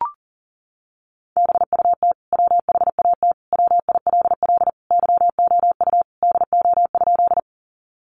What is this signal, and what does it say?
Morse code, heard as a keyed tone, about 40 wpm, 700 Hz. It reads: BUT WHAT WILL YOU DO?
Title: Sentences from Top 100 Words 40wpm